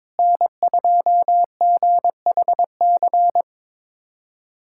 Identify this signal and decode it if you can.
Morse code, heard as a keyed tone, about 22 wpm, 700 Hz. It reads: N2GHC
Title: USA Callsign DE Callsign - 5 Characters - 22wpm